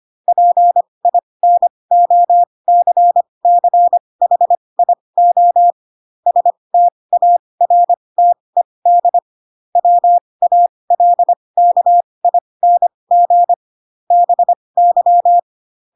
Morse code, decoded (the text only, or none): PINOCCHIO STARTED WALKING BY